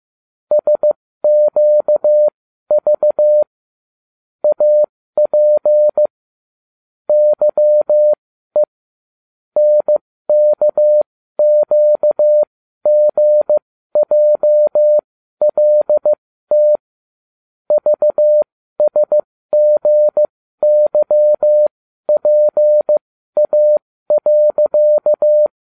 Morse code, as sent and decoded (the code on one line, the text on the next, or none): ... --.- ...- / .- .--. / -.-- . / -. -.- --.- --. .--- .-.. - / ...- ... --. -.-- .--. .- .-.-.-
SQV AP YE NKQGJLT VSGYPA.